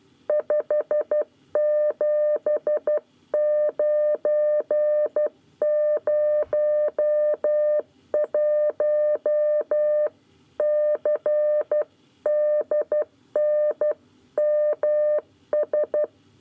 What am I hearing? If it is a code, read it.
Morse code, heard as a keyed tone, about 12 wpm, 600 Hz. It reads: 57901 CDNMS
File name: 9 lezione cw 1